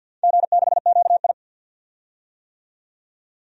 Morse code, heard as a keyed tone, about 47 wpm, 700 Hz. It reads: G6QI
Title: DXCC-100 Call Signs - 4 Characters - 50wpm